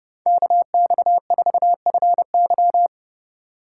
Morse code, heard as a keyed tone, about 29 wpm, 700 Hz. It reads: KX4FY
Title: USA Call Signs - 5 Characters - 8x Character Spacing - 30wpm